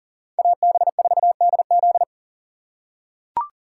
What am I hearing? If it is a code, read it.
Morse code, heard as a keyed tone, about 40 wpm, 700 Hz. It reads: AB4DZ
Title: USA Call Signs - 3x Character Spacing - 40wpm